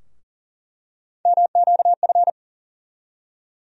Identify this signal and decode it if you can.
Morse code, heard as a keyed tone, about 40 wpm, 700 Hz. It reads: MQF
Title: USA Call Sign Postfixes - 3 Letters 40wpm